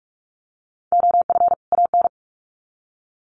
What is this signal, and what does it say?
Morse code, heard as a keyed tone, about 44 wpm, 700 Hz. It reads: OF AN
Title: Sets of 2 Words 45wpm